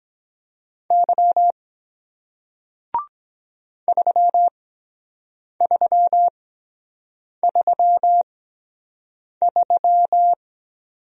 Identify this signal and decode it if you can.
Morse code, heard as a keyed tone, about 23 wpm, spaced as for 7 wpm, 700 Hz. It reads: Y 3333